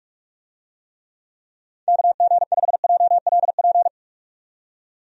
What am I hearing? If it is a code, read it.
Morse code, heard as a keyed tone, about 45 wpm, 700 Hz. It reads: KG5JLP
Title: USA Call Signs - 6 Characters - 45wpm